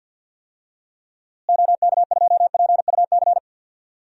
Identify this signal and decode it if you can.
Morse code, heard as a keyed tone, about 48 wpm, 700 Hz. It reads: OK2PUC
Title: DXCC-100 Call Signs - 6 Characters - Repeated 3x - 50wpm